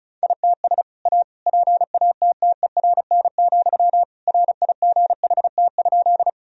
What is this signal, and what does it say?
Morse code, heard as a keyed tone, about 34 wpm, 700 Hz. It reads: ITS A PATTERN, RIGHT?
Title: Sentences from Top 1000 Words - Review Part 2 - 35wpm